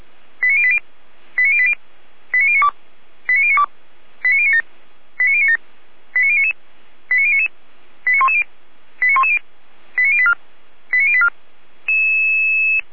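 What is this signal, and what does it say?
Five-tone selective call (ZVEI), heard as quick runs of five tones, tones of about 70 ms. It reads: 89080 89080 89092 89092 89097 89097 89099 89099 89110 89110 89084 89084
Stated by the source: ZVEI five-tone selective call received on an SDR, from the feuerwehr-orlamuende.org fire brigade server.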